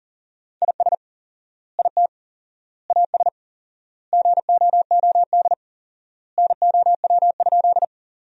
Morse code, heard as a keyed tone, about 40 wpm, 700 Hz. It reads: IS IT AS GOOD NOW?
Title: 4x Word-Spacing - Sentences from Top 200 Words - 40wpm